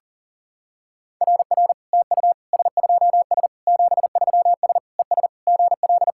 Morse code, heard as a keyed tone, about 40 wpm, 700 Hz. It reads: RR TU S2S 73S ES GL
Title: SOTA QSO Practice 40wpm